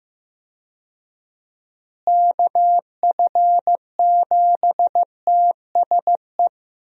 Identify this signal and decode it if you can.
Morse code, heard as a keyed tone, about 15 wpm, 700 Hz. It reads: KF7TSE
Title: USA Call Signs - 6 Characters - Repeated 3x - 15wpm